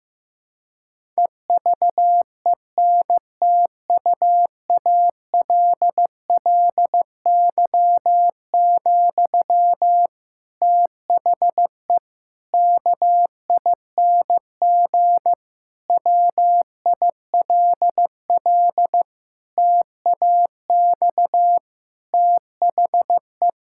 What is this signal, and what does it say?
Morse code, heard as a keyed tone, about 16 wpm, 700 Hz. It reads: EVENTUALLY, THE KING WILL TAX THE
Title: Sentences from Top 1000 Words - Review Part 6 - 15wpm